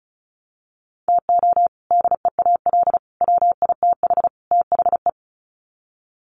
Morse code, heard as a keyed tone, about 34 wpm, 700 Hz. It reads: TO DEAL WITH THE
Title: Sets of 4 Words 35wpm